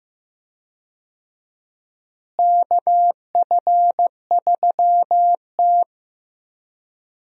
Morse code, heard as a keyed tone, about 16 wpm, 700 Hz. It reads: KF3T
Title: USA Call Signs - 4 Characters - Repeated 3x - 15wpm